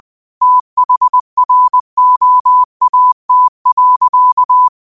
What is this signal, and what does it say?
Morse code, heard as a keyed tone, about 20 wpm, 1000 Hz. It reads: THROAT.